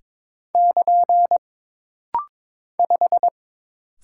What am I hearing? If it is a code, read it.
Morse code, heard as a keyed tone, about 22 wpm, 700 Hz. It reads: <KN> 5
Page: This rapid-fire timing is geared to push your ICR (Instant Character Recognition) skills.